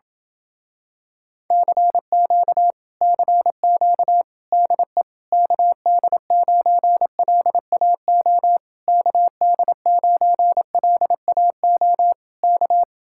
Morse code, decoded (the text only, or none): CQ CQ DE KD9LAO KD9LAO K